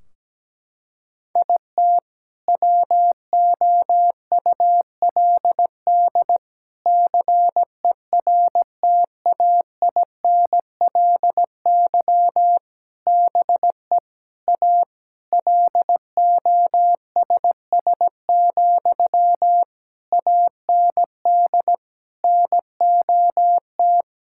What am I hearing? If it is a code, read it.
Morse code, heard as a keyed tone, about 17 wpm, 700 Hz. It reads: IT WOULD CERTAINLY BE A LOSS, AND NOT
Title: Sentences from Top 1600 Words 17wpm